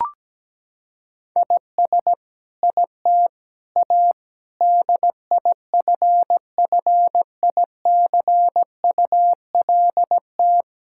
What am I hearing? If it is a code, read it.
Morse code, heard as a keyed tone, about 17 wpm, 700 Hz. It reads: IS IT A DIFFICULT
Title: Sentences from Top 1000 Words - Review Part 6 - 17wpm